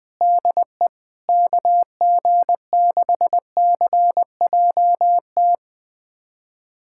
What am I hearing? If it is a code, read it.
Morse code, heard as a keyed tone, about 21 wpm, 700 Hz. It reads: DE KG6CJT
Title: USA Callsign DE Callsign 20wpm